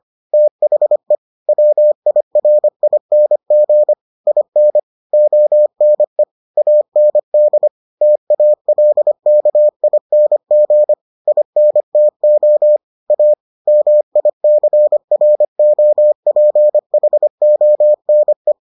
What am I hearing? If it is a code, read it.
Morse code, heard as a keyed tone, about 25 wpm, 600 Hz. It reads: THE WIRING IN ONE AND TALKING INTO A MICROPHONE